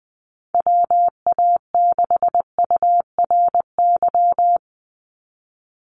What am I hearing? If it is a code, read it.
Morse code, heard as a keyed tone, about 20 wpm, 700 Hz. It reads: WA6URY
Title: Call Signs - 20wpm